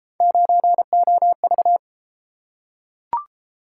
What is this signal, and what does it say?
Morse code, heard as a keyed tone, about 28 wpm, 700 Hz. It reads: 9OV